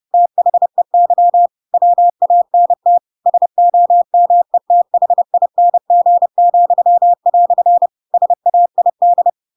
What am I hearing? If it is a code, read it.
Morse code, heard as a keyed tone, about 30 wpm, 700 Hz. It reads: THEY WANT SOMETHING," SAID
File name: Pinocchio / Pinocchio ch8 30 30 ik1hge